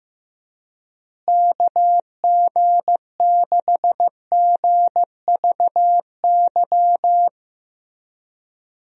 Morse code, heard as a keyed tone, about 16 wpm, 700 Hz. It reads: KG6GVY
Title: USA Call Signs - 6 Characters - Repeated 3x - 15wpm